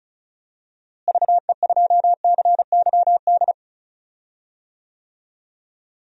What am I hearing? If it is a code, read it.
Morse code, heard as a keyed tone, about 36 wpm, 700 Hz. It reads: VE2CYD
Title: DXCC-100 Call Signs - 6 Characters - Repeated 2x - 35wpm